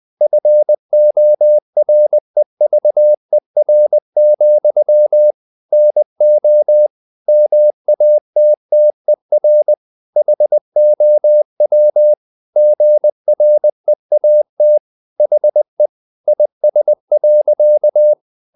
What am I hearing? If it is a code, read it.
Morse code, heard as a keyed tone, about 20 wpm, 600 Hz. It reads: FOREVER, NO MATTER HOW GREAT HE IS.
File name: the old man and the sea / the old man and the sea part 1 20wpm